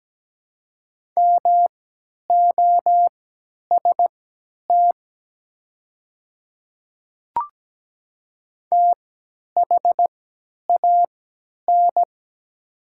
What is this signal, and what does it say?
Morse code, heard as a keyed tone, about 17 wpm, spaced as for 10 wpm, 700 Hz. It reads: MOST THAN
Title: 3x Character Spacing - Top 100 Words - 17wpm